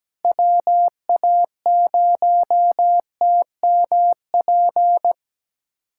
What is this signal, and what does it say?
Morse code, heard as a keyed tone, about 17 wpm, 700 Hz. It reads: WA0TMP
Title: USA Callsign DE Callsign 17wpm